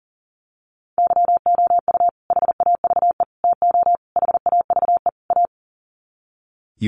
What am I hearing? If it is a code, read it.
Morse code, heard as a keyed tone, about 40 wpm, 700 Hz. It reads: YOU HAVE TO HAVE A
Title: Sets of 5 Words Encore 40wpm